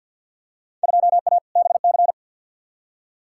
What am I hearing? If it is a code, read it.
Morse code, heard as a keyed tone, about 47 wpm, 700 Hz. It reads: 2A BC
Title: Field Day Exchanges - 50wpm